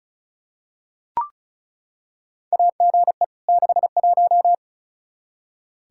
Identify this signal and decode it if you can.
Morse code, heard as a keyed tone, about 36 wpm, 700 Hz. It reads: AGE 61
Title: K1USN SST QSO Practice 35wpm